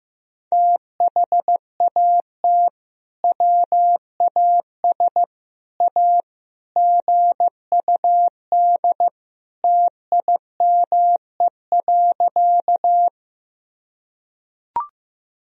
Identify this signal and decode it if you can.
Morse code, heard as a keyed tone, about 15 wpm, 700 Hz. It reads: THAT WAS A GUD TIME.
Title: Sentences from Top 200 Words and Abbreviations 15wpm